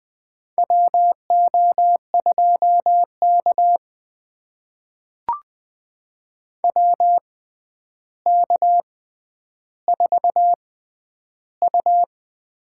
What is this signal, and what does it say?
Morse code, heard as a keyed tone, about 20 wpm, spaced as for 10 wpm, 700 Hz. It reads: WO2K WK4U
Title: USA Call Signs - 6x Character Spacing - 20wpm